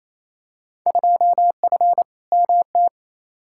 Morse code, heard as a keyed tone, about 28 wpm, 700 Hz. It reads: 2F MT
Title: Field Day Exchanges Encore - 28wpm